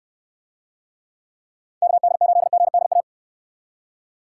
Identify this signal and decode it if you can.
Morse code, heard as a keyed tone, about 63 wpm, 700 Hz. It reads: KD8GDA